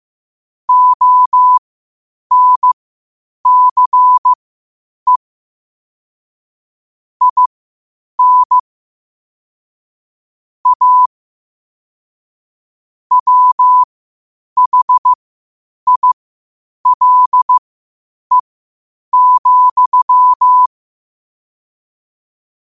Morse code, heard as a keyed tone, about 15 wpm, spaced as for 8 wpm, 1000 Hz. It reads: ONCE IN A WHILE,
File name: qotdfiles / 8wpm